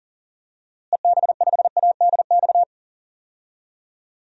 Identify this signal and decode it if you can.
Morse code, heard as a keyed tone, about 40 wpm, 700 Hz. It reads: EB5ADX
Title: DXCC-100 Call Signs - 6 Characters - Repeated 2x - 40wpm